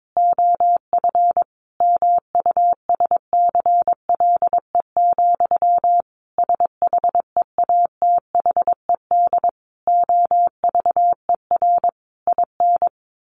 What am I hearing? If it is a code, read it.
Morse code, heard as a keyed tone, about 22 wpm, 700 Hz. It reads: OF MUSCLE, SHEATHED OVER IN
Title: A Sound Of Thunder 22wpm